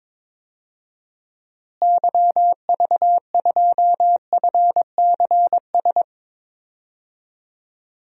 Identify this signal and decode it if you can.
Morse code, heard as a keyed tone, about 22 wpm, 700 Hz. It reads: YV2FCS
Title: DXCC-100 Call Signs - 6 Characters - Repeated 2x - 22wpm